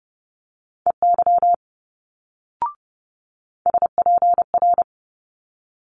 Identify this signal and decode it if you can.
Morse code, heard as a keyed tone, about 30 wpm, 700 Hz. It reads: EY SPR
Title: Multi-letter Phonemes Quick Fire 30wpm